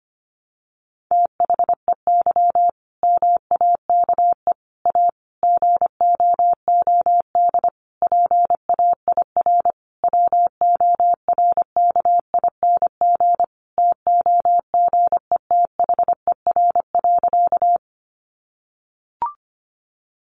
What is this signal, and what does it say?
Morse code, heard as a keyed tone, about 25 wpm, 700 Hz. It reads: THEY MAKE A GOOD PAIR WORKING TOGETHER.
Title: Sentences from Top 1200 Words 25wpm